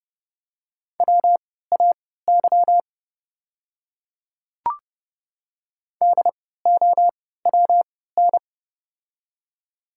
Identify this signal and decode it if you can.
Morse code, heard as a keyed tone, about 31 wpm, spaced as for 17 wpm, 700 Hz. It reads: WAY DOWN
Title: 3x Character Spacing - Top 100 Words - 30wpm